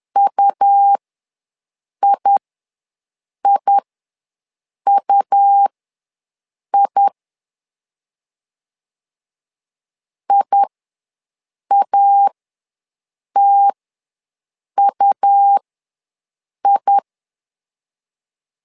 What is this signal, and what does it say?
Morse code, heard as a keyed tone, about 11 wpm, spaced as for 6 wpm, 800 Hz. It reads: UIIUI IATUI
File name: LEZIONI / 2 UHDN / 2 U ws 10wpm 2017